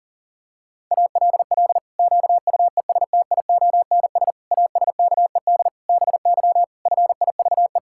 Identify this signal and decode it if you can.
Morse code, heard as a keyed tone, about 40 wpm, 700 Hz. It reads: ALL QUESTIONS ASKED BY FIVE
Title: Pangram Challenge v2 - 40wpm